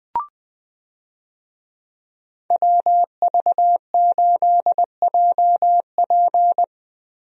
Morse code, heard as a keyed tone, about 21 wpm, 700 Hz. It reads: WV8JP
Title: USA Call Signs - 5 Characters - Repeated 3x - 20wpm